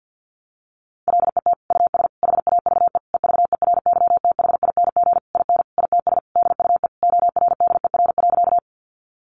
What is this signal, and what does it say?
Morse code, heard as a keyed tone, about 50 wpm, 700 Hz. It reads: LET US HAVE EVERYTHING IN ITS DUE ORDER.
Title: Sentences from Top 900 Words 50wpm